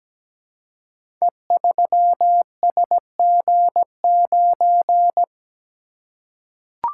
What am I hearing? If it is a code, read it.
Morse code, heard as a keyed tone, about 17 wpm, 700 Hz. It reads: E3SG9
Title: Code Groups - 5 Letter:Numbers - 17wpm